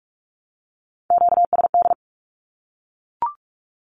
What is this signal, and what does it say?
Morse code, heard as a keyed tone, about 44 wpm, 700 Hz. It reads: QSD
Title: Q-Codes Repeated 3x 45wpm